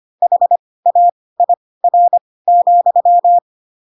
Morse code, heard as a keyed tone, about 25 wpm, spaced as for 18 wpm, 700 Hz. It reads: HAIR,
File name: Pinocchio ch3 12 25 ik1hge